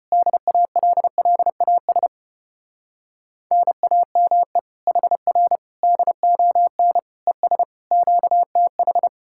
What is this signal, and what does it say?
Morse code, heard as a keyed tone, about 33 wpm, 700 Hz. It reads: DALLAS NAME HR DON ES QTH